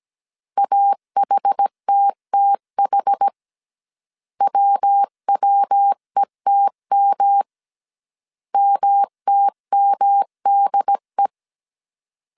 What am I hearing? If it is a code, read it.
Morse code, heard as a keyed tone, about 17 wpm, 800 Hz. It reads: AHTTH WWETM MTMDE
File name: CW / DET / 1 3 1500C S A 16wpm 2141